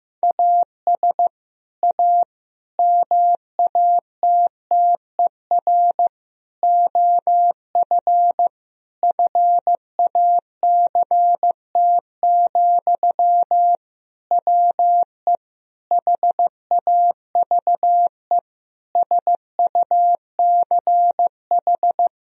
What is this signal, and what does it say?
Morse code, heard as a keyed tone, about 15 wpm, 700 Hz. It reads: AS A MATTER OF FACT, WE HAVE SUCH
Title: Sentences from Top 1000 Words - Review Part 3 - 15wpm